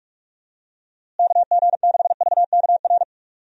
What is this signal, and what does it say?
Morse code, heard as a keyed tone, about 46 wpm, 700 Hz. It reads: KG6VKR
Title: USA Call Signs - 6 Characters - Repeated 3x - 45wpm